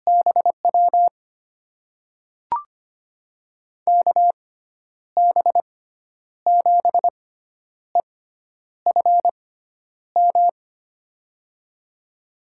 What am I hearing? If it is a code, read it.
Morse code, heard as a keyed tone, about 25 wpm, spaced as for 10 wpm, 700 Hz. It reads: BW KB7EFM
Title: USA Call Signs - 6x Character Spacing - 25wpm